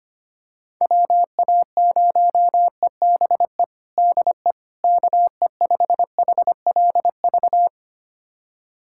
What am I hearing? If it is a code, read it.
Morse code, heard as a keyed tone, about 25 wpm, 700 Hz. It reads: WA0EBE DE KE5HLV
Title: USA Callsign DE Callsign - 6 Characters - 25wpm